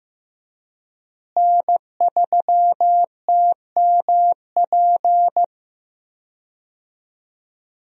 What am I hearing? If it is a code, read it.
Morse code, heard as a keyed tone, about 15 wpm, 700 Hz. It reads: N3TMP
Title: USA Call Signs - 5 Characters - Repeated 2x - 15wpm